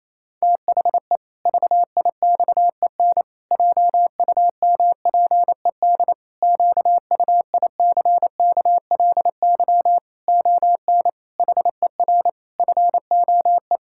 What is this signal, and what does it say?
Morse code, heard as a keyed tone, about 28 wpm, 700 Hz. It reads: THE VIXEN JUMPED QUICKLY ON HER FOE
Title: Pangram Challenge v2 - 28wpm